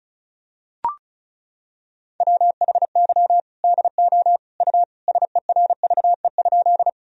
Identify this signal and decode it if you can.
Morse code, heard as a keyed tone, about 34 wpm, 700 Hz. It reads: WHY DO U SERVE?
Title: Sentences from Top 400 Words and Abbreviations 35wpm